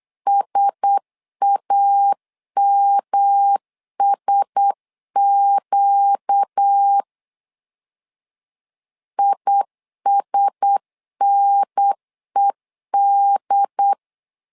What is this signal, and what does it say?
Morse code, heard as a keyed tone, about 8 wpm, 800 Hz. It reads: SAMSQ ISNED
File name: LEZIONI / 4 Q1S 8wpm 2710